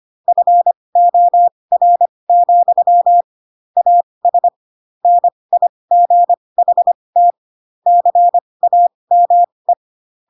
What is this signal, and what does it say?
Morse code, heard as a keyed tone, about 25 wpm, spaced as for 20 wpm, 700 Hz. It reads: FOR, AS NIGHT CAME
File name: Pinocchio / Pinocchio ch5 15 25 ik1hge